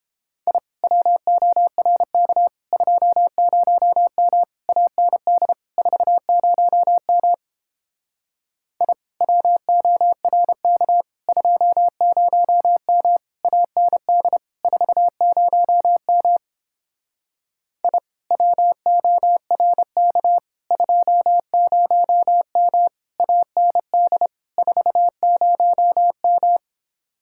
Morse code, heard as a keyed tone, about 29 wpm, 700 Hz. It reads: I WORK 20M AND 40M I WORK 20M AND 40M I WORK 20M AND 40M